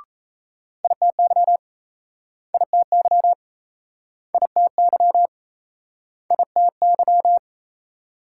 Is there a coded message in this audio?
Yes: Morse code from 0.8 to 7.4 s, heard as a keyed tone, about 38 wpm, 700 Hz. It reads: ITY ITY ITY ITY